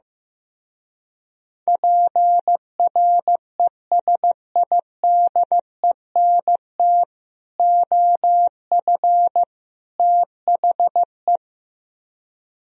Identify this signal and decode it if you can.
Morse code, heard as a keyed tone, about 15 wpm, 700 Hz. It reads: PRESIDENT OF THE
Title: Speed-Racing - Sets of 3 Words 15wpm